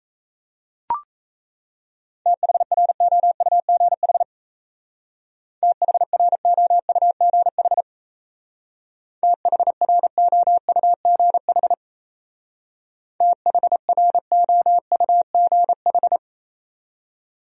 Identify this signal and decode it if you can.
Morse code, heard as a keyed tone, about 37 wpm, 700 Hz. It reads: THROUGH THROUGH THROUGH THROUGH